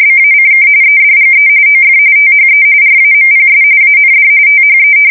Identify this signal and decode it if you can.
RTTY, about 45.45 baud. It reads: BACK EDGE OF THE RAMPS AND THE
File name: fsk170 / battleshipnj